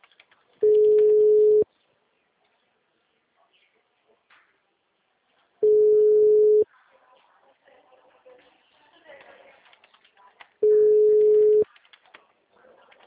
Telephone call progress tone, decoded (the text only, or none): ringback tone